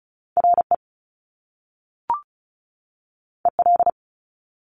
Morse code, heard as a keyed tone, about 36 wpm, 700 Hz. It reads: RE EL
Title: Top Two Letter Combinations in Words 35wpm